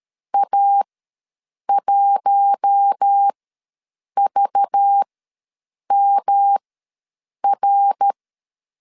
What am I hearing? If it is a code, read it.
Morse code, heard as a keyed tone, about 13 wpm, spaced as for 7 wpm, 800 Hz. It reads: A1VMR
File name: LEZIONI / 5 LBVC / 5 LBVC ws 12wpm 3159